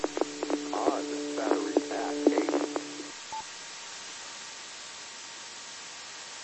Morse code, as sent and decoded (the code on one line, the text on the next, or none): none